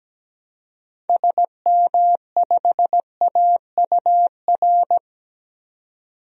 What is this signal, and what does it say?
Morse code, heard as a keyed tone, about 17 wpm, 700 Hz. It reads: SM5AUR
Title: DXCC-100 Call Signs - 6 Characters - 17wpm